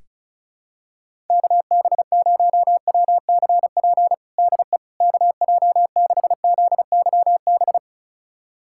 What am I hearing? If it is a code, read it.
Morse code, heard as a keyed tone, about 36 wpm, 700 Hz. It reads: KB0WCP DE KJ6ZYB
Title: USA Callsign DE Callsign - 6 Characters - 35wpm